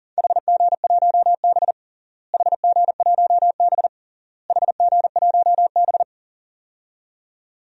Morse code, heard as a keyed tone, about 37 wpm, 700 Hz. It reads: HG1B HG1B HG1B
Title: DXCC-100 Call Signs - 4 Characters - Repeated 3x - 40wpm